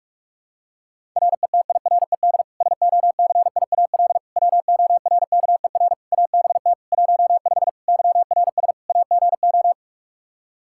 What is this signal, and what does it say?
Morse code, heard as a keyed tone, about 46 wpm, 700 Hz. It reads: RETIRED SOCIAL WORKER ABT 15 YRS AGO
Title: Practice QSO - Exchange 3 - More Info - 45wpm